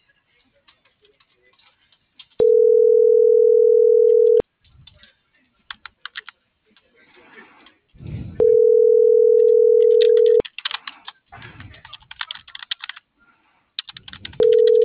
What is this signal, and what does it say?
Ringback tone (North America), 440 + 480 Hz.